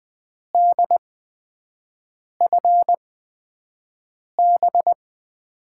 Morse code, heard as a keyed tone, about 20 wpm, spaced as for 5 wpm, 700 Hz. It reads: DFB
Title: USA Call Signs - 8x Character Spacing - 20wpm